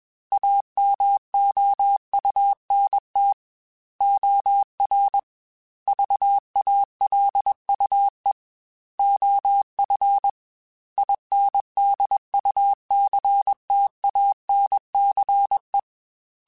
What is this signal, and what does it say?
Morse code, heard as a keyed tone, about 21 wpm, 800 Hz. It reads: AMOUNT OR VALUE OF INDUCTANCE